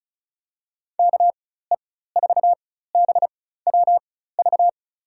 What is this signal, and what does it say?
Morse code, heard as a keyed tone, about 34 wpm, spaced as for 16 wpm, 700 Hz. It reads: KE4BWV
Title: USA Call Signs - 4x Character Spacing - 35wpm